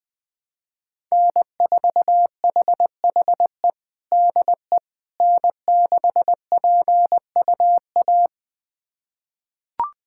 Morse code, heard as a keyed tone, about 20 wpm, 700 Hz. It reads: N4HHE DE N6PUA
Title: USA Callsign DE Callsign - 5 Characters - 20wpm